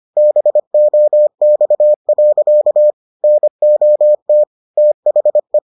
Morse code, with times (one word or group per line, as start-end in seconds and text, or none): 0.2-2.9 BOX.
3.2-4.4 NOT
4.8-5.6 THE